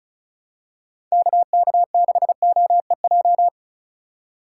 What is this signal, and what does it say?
Morse code, heard as a keyed tone, about 34 wpm, 700 Hz. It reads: KK6OEJ
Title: USA Call Signs - 6 Characters - 4x Character Spacing - 35wpm